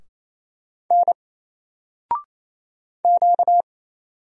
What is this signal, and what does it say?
Morse code, heard as a keyed tone, about 27 wpm, 700 Hz. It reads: N Q
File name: Single-Letter-Number-28wpm